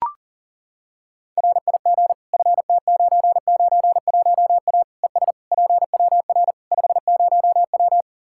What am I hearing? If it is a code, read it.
Morse code, heard as a keyed tone, about 40 wpm, 700 Hz. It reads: RIG FT991A ES PWR 50W
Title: Practice QSO - Exchange 2 - Info - 40wpm